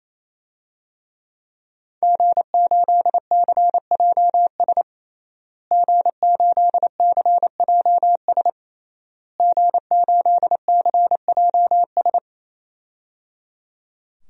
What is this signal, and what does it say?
Morse code, heard as a keyed tone, about 28 wpm, 700 Hz. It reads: G8CJS G8CJS G8CJS
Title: DXCC-100 Call Signs - 5 Characters - Repeated 3x - 28wpm